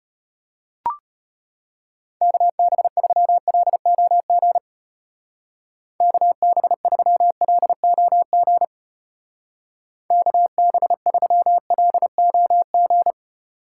Morse code, heard as a keyed tone, about 32 wpm, 700 Hz. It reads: KB3LOG KB3LOG KB3LOG